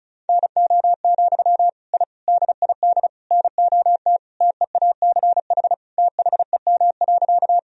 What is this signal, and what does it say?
Morse code, heard as a keyed tone, about 34 wpm, 700 Hz. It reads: NO, I DID NOT TEACH THEM.
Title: Sentences from Top 1000 Words - Review Part 2 - 35wpm